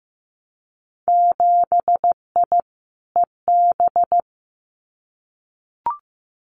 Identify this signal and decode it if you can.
Morse code, heard as a keyed tone, about 15 wpm, 700 Hz. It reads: 7I EB
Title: Winter Field Day Exchanges Encore 15wpm